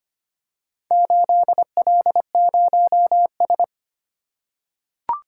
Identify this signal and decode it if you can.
Morse code, heard as a keyed tone, about 25 wpm, 700 Hz. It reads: 8L0S
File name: practice / 25wpm / Four Letter-Number Combinations - 25wpm